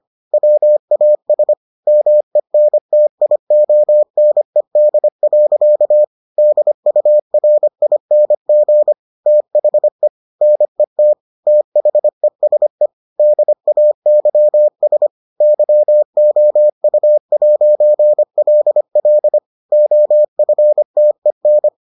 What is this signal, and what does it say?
Morse code, heard as a keyed tone, about 25 wpm, 600 Hz. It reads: WAS MENTIONED. DURING THE NET THESE DAYS YOU'LL OFTEN